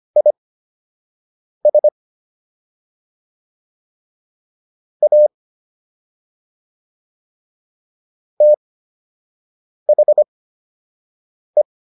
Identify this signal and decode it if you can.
Morse code, heard as a keyed tone, about 25 wpm, spaced as for 6 wpm, 600 Hz. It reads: IS A THE